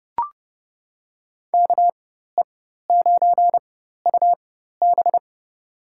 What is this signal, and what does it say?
Morse code, heard as a keyed tone, about 31 wpm, spaced as for 14 wpm, 700 Hz. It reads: KE9UB
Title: USA Call Signs - 5 Characters - 4x Character Spacing - 30wpm